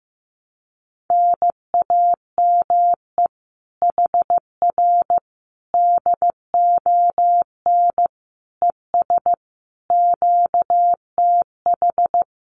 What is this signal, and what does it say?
Morse code, heard as a keyed tone, about 14 wpm, 700 Hz. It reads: NAME HR DON ES QTH
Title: Common QSO Phrases Encore 15wpm